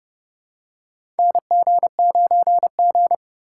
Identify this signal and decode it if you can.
Morse code, heard as a keyed tone, about 30 wpm, 700 Hz. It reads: NG9G
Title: USA Call Signs - 4 Characters - 4x Character Spacing - 30wpm